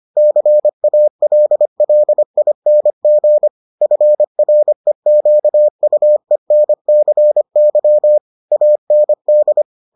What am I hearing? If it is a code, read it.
Morse code, heard as a keyed tone, about 25 wpm, 600 Hz. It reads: CALLING FREQUENCY AND